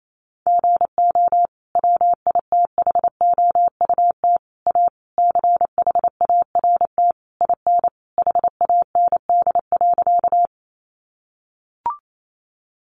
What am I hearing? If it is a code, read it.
Morse code, heard as a keyed tone, about 28 wpm, 700 Hz. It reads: GO WITHOUT A CHART IN HAND.
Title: Sentences from Top 1300 Words 28wpm